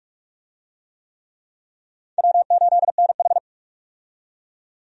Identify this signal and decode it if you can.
Morse code, heard as a keyed tone, about 46 wpm, 700 Hz. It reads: W8NH
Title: USA Call Signs - 4 Characters - 45wpm